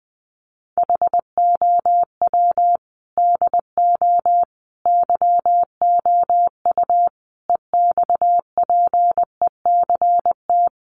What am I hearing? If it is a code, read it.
Morse code, heard as a keyed tone, about 20 wpm, 700 Hz. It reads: HOW DO YOU EXPECT
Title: Sentences from Top 1100 Words 20wpm